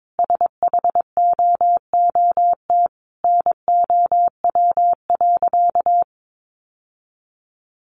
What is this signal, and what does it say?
Morse code, heard as a keyed tone, about 22 wpm, 700 Hz. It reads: SHOOT NOW.
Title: Sentences from Top 1200 Words 22wpm